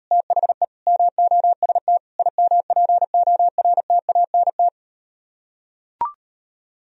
Morse code, heard as a keyed tone, about 38 wpm, 700 Hz. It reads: THE MOST IMPORTANT